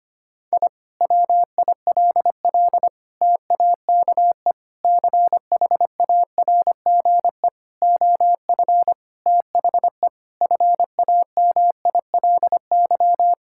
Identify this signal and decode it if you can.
Morse code, heard as a keyed tone, about 25 wpm, 700 Hz. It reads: I WILL TAKE CHARGE OF THE FAMILY
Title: Sentences from Top 1000 Words - Review Part 6 - 25wpm